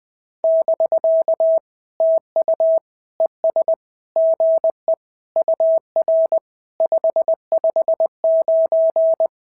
Morse code, heard as a keyed tone, about 21 wpm, 690 Hz. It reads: <BK> TU ES GE UR 559
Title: SOTA QSO Practice 20wpm